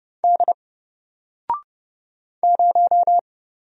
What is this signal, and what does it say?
Morse code, heard as a keyed tone, about 30 wpm, 700 Hz. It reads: D 0
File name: Single-Letter-Number-30wpm